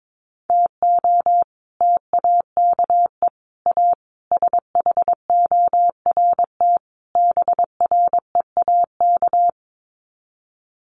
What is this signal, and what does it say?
Morse code, heard as a keyed tone, about 22 wpm, 700 Hz. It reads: TO TAKE A SHORT BREAK
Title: Sets of 5 Words Encore2 22wpm